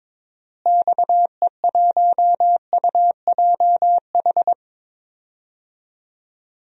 Morse code, heard as a keyed tone, about 22 wpm, 700 Hz. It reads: XE1UJH
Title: DXCC-100 Call Signs - 6 Characters - Repeated 3x - 22wpm